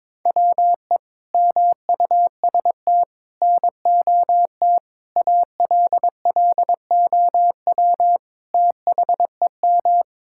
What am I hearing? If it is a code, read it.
Morse code, heard as a keyed tone, about 22 wpm, 700 Hz. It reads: WE MUST NOT ALLOW THEM
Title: Sentences from Top 1000 Words - Review Part 5 - 22wpm